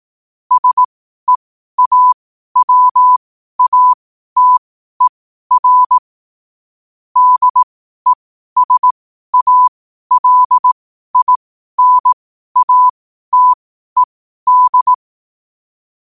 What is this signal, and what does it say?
Morse code, heard as a keyed tone, about 18 wpm, spaced as for 12 wpm, 1000 Hz. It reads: SEAWATER DESALINATED
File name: qotdfiles / 12wpm